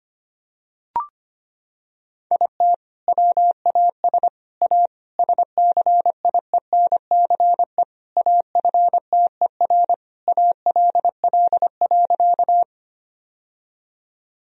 Morse code, heard as a keyed tone, about 25 wpm, 700 Hz. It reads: IT WAS A SCIENCE AFTER ALL.
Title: Sentences from Top 1000 Words - Review Part 2 - 25wpm